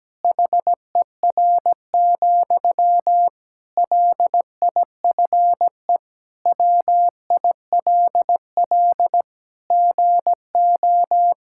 Morse code, heard as a keyed tone, about 17 wpm, 700 Hz. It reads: HER, LIFE WILL GO
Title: Sentences from Top 1600 Words 17wpm